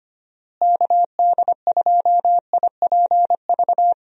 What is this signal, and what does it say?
Morse code, heard as a keyed tone, about 25 wpm, 700 Hz. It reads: KD2IPV
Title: USA Call Signs - 6 Characters - Repeated 3x - 25wpm